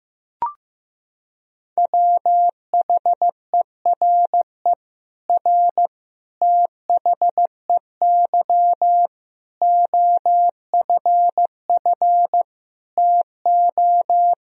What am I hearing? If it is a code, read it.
Morse code, heard as a keyed tone, about 15 wpm, 700 Hz. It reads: WHERE R THEY OFF TO
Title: Sentences from Top 200 Words and Abbreviations 15wpm